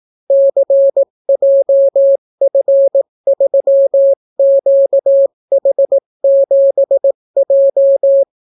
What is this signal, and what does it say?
Morse code, heard as a keyed tone, about 18 wpm, 550 Hz. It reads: CJF3QH7J